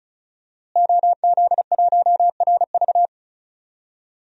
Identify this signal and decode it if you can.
Morse code, heard as a keyed tone, about 36 wpm, 700 Hz. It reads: OZ1RV
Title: DXCC-100 Call Signs - 5 Characters - Repeated 2x - 35wpm